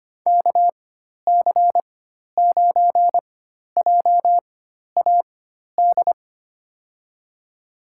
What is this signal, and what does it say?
Morse code, heard as a keyed tone, about 25 wpm, spaced as for 12 wpm, 700 Hz. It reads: KC9JAD
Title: USA Call Signs - 6 Characters - 4x Character Spacing - 25wpm